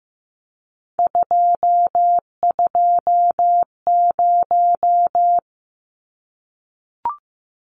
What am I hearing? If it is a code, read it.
Morse code, heard as a keyed tone, about 15 wpm, 700 Hz. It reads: 220
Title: Numbers - 3-Digit 15wpm